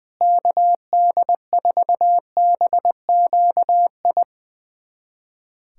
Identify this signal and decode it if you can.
Morse code, heard as a keyed tone, about 20 wpm, 700 Hz. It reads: KD4BQI
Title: USA Callsign DE Callsign - 6 Characters - 20wpm